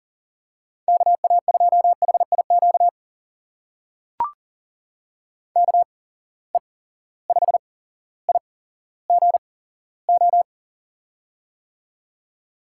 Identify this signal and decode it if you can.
Morse code, heard as a keyed tone, about 39 wpm, spaced as for 17 wpm, 700 Hz. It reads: KA2HIQ KE5IGO
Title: USA Call Signs - 6 Characters - 8x Character Spacing - 40wpm